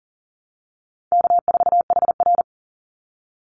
Morse code, heard as a keyed tone, about 40 wpm, 700 Hz. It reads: K4HR
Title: Easy Call Signs - 40wpm